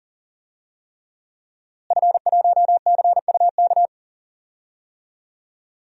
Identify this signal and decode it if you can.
Morse code, heard as a keyed tone, about 40 wpm, 700 Hz. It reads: F1CUK
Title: DXCC-100 Call Signs - 5 Characters - Repeated 3x - 40wpm